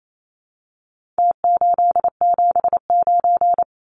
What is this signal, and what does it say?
Morse code, heard as a keyed tone, about 28 wpm, 700 Hz. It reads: T879
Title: Numbers - Contest Style - 4-Digit 28wpm